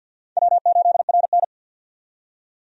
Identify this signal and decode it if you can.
Morse code, heard as a keyed tone, about 51 wpm, 700 Hz. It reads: W8RN
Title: USA Call Signs - 4 Characters - 50wpm